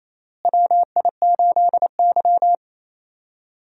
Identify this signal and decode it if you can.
Morse code, heard as a keyed tone, about 28 wpm, 700 Hz. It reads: WI8Y
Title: USA Call Signs - 4 Characters - Repeated 3x - 28wpm